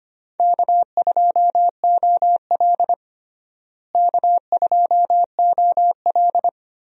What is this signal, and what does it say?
Morse code, heard as a keyed tone, about 25 wpm, 700 Hz. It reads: K2OL K2OL
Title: USA Call Signs - 4 Characters - Repeated 3x - 25wpm